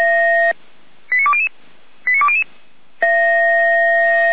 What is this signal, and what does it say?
Five-tone selective call (ZVEI), heard as quick runs of five tones, tones of about 70 ms. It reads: 89200 89200
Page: SDR/ZVEI